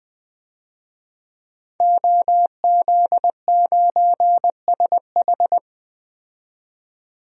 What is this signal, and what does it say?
Morse code, heard as a keyed tone, about 20 wpm, 700 Hz. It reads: OZ9SH
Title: DXCC-100 Call Signs - 5 Characters - 20wpm